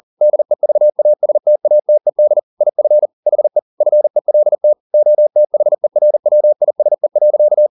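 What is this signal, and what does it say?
Morse code, heard as a keyed tone, about 40 wpm, 600 Hz. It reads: DEVASTATED IF HE FELT OTHERWISE.
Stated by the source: Morse code at 40 wpm, practice text, CW med Gnister.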